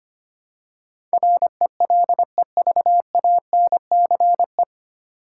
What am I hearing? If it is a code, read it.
Morse code, heard as a keyed tone, about 25 wpm, 700 Hz. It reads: RELEVANCE
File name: practice / 25wpm / 9 Letter Words - First 3 Letters Given - 25wpm